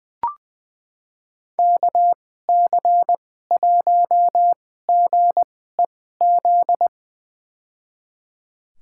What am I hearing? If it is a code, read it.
Morse code, heard as a keyed tone, about 21 wpm, spaced as for 15 wpm, 700 Hz. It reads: KC1GEZ
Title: USA Call Signs - 6 Characters - 2x Character Spacing - 20wpm